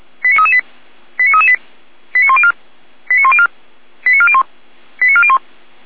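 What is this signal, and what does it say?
Five-tone selective call (ZVEI), heard as quick runs of five tones, tones of about 70 ms. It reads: 89338 89338 89104 89104 89401 89401
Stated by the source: ZVEI five-tone selective call 89080 received by SDR.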